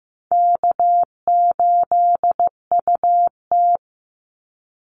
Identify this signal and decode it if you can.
Morse code, heard as a keyed tone, about 15 wpm, 700 Hz. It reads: K8UT
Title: Easy Call Signs Repeated 2x - 15wpm